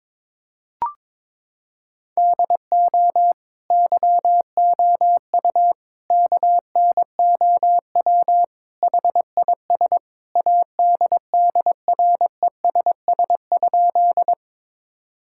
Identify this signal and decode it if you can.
Morse code, heard as a keyed tone, about 22 wpm, 700 Hz. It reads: DO YOU KNOW HIS ADDRESS?
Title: Sentences from Top 1000 Words - Review Part 6 - 22wpm